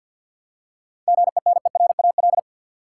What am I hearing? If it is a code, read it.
Morse code, heard as a keyed tone, about 50 wpm, 700 Hz. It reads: GENERAL
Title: Seven Letter Words - 50wpm